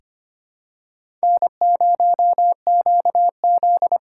Morse code, heard as a keyed tone, about 25 wpm, 700 Hz. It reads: N0QZ
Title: USA Call Signs - 4 Characters - 6x Character Spacing - 25wpm